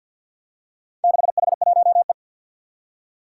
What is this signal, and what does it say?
Morse code, heard as a keyed tone, about 52 wpm, 700 Hz. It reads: 6H1E
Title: Four Challenging Letter-Number Combinations 50wpm